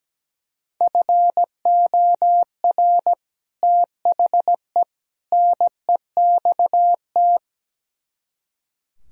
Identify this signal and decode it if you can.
Morse code, heard as a keyed tone, about 17 wpm, 700 Hz. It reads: FOR THE NEXT
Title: Speed-Racing - Sets of 3 Words 17wpm